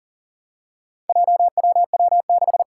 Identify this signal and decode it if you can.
Morse code, heard as a keyed tone, about 40 wpm, 700 Hz. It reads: JWW6
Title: Four Challenging Letter-Number Combinations 40wpm